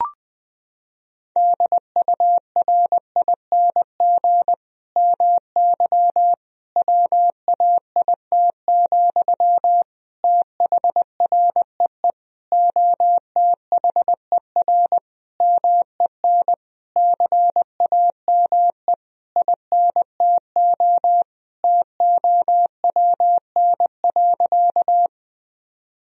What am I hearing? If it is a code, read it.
Morse code, heard as a keyed tone, about 20 wpm, 700 Hz. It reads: DURING MY WAIT, THREE OTHER MEN CAME INTO TOWN.
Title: Sentences from Top 1000 Words - Review Part 2 - 20wpm